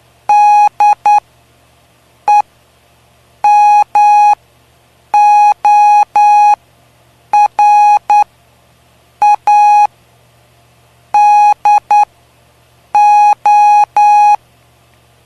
Morse code, heard as a keyed tone, about 9 wpm, spaced as for 6 wpm, 830 Hz. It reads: DEMORADO
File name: cw / py2lcd / 05-PY2LCD-Li��o 5